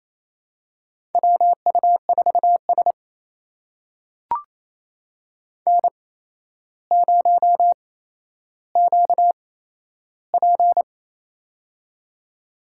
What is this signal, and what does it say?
Morse code, heard as a keyed tone, about 27 wpm, spaced as for 12 wpm, 700 Hz. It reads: WU4S N0QP
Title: USA Call Signs - 4 Characters - 8x Character Spacing - 28wpm